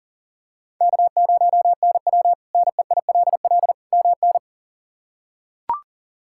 Morse code, heard as a keyed tone, about 37 wpm, 700 Hz. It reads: K0NW NEILL MN
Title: K1USN SST QSO Practice - 40wpm